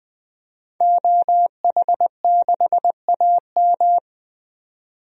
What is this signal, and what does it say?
Morse code, heard as a keyed tone, about 20 wpm, 700 Hz. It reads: OH6AM
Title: DXCC-100 Call Signs - 5 Characters - 20wpm